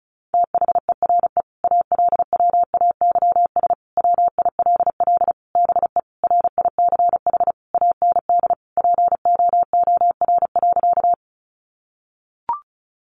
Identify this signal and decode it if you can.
Morse code, heard as a keyed tone, about 34 wpm, 700 Hz. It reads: THERE ALWAYS WILL BE RICH AND POOR.
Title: Sentences from Top 1100 Words 35wpm